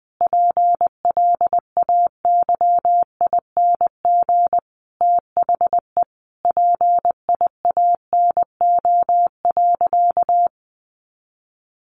Morse code, heard as a keyed tone, about 20 wpm, 700 Hz. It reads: PLAYING THE PIANO.
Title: Additional Easy Phrases 20wpm